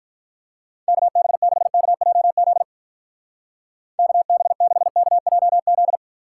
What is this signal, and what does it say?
Morse code, heard as a keyed tone, about 44 wpm, 700 Hz. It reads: KB6KJZ KB6KJZ